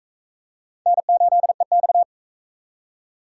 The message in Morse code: N8EX

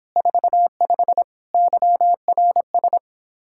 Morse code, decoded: 45 YRS

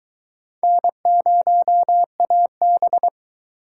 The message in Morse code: N0AB

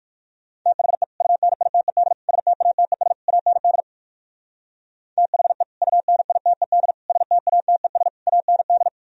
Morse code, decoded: THE UNITED STATES AND THE UNITED STATES AND

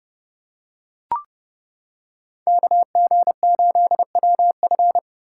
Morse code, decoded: KG8WF

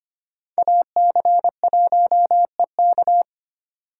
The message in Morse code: AC1EK

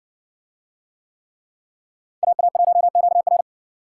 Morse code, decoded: AA1QR

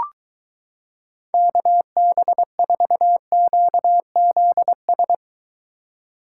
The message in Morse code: KB4QZS

